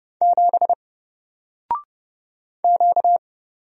Morse code, heard as a keyed tone, about 30 wpm, 700 Hz. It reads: 7 Q
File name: Single-Letter-Number-30wpm